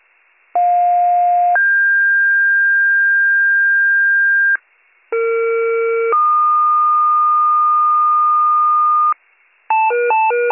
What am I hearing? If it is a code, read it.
Two-tone page, heard as a one-second tone, then a three-second tone, about 710 then 1600 Hz, about 460 then 1220 Hz, with noise. It is heard from 0.6 to 9.1 s.